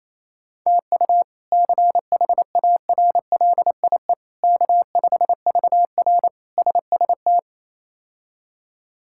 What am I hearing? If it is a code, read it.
Morse code, heard as a keyed tone, about 28 wpm, 700 Hz. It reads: TU CHARLIE K5VR SST
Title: K1USN SST QSO Practice - 28wpm